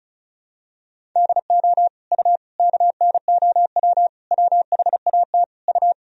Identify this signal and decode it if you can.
Morse code, heard as a keyed tone, about 36 wpm, 700 Hz. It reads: DO U KNOW WHAT U
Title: Sentences from Top 100 Words and Abbreviations 35wpm